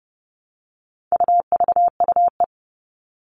Morse code, heard as a keyed tone, about 30 wpm, 700 Hz. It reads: UVUE
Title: All Cut Numbers - 4-Digit 30wpm